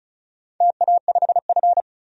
Morse code, heard as a keyed tone, about 34 wpm, 700 Hz. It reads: TA5F
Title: DXCC-100 Call Signs - 4 Characters - 35wpm